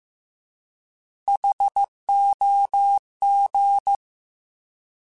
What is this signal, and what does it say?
Morse code, heard as a keyed tone, about 15 wpm, 800 Hz. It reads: HOG
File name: Alphabetical / Alphabetical mp3s / Hs @13